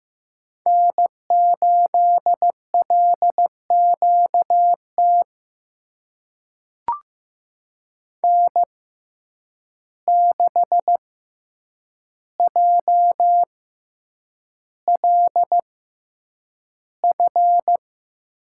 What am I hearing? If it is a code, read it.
Morse code, heard as a keyed tone, about 15 wpm, spaced as for 8 wpm, 700 Hz. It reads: N8LQT N6JLF
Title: USA Call Signs - 5 Characters - 6x Character Spacing - 15wpm